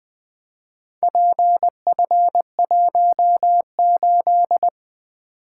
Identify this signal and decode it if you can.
Morse code, heard as a keyed tone, about 20 wpm, 700 Hz. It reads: PF18
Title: Four Letter-Number Combinations - 20wpm